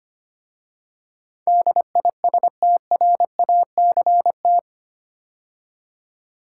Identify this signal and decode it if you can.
Morse code, heard as a keyed tone, about 25 wpm, 700 Hz. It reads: DISTRACT
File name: 8 Letter Words - First 3 Letters Given - 25wpm